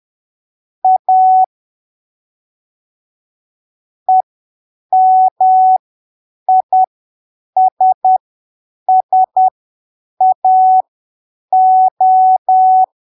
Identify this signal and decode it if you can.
Morse code, heard as a keyed tone, about 10 wpm, spaced as for 7 wpm, 750 Hz. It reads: A EMISSAO
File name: arquivos / texto / texto 18 10ppm